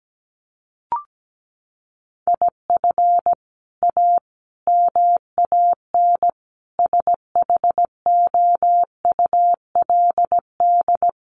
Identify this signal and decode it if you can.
Morse code, heard as a keyed tone, about 17 wpm, 700 Hz. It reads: IF A MAN SHOULD
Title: Sentences from Top 1500 Words 17wpm